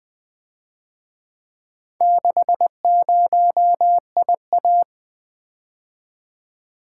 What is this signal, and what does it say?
Morse code, heard as a keyed tone, about 24 wpm, 700 Hz. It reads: 60IA
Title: Speed-Racing - Four Letter-Number Combinations - 20wpm